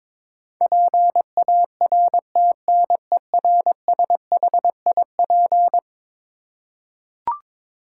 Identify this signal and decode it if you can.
Morse code, heard as a keyed tone, about 22 wpm, 700 Hz. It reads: PARTNERSHIP
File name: practice / 22wpm / Eleven Letter Words - 22wpm